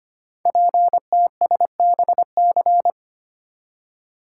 Morse code, heard as a keyed tone, about 25 wpm, 700 Hz. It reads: PTSBC
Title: Code Groups - 5 Letters - 25wpm